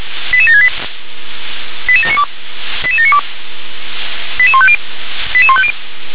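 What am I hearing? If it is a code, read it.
Five-tone selective call (ZVEI), heard as quick runs of five tones, tones of about 70 ms. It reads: 88768 88792 88792 88150 88150
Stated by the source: SDR-received ZVEI five-tone call, code 88658.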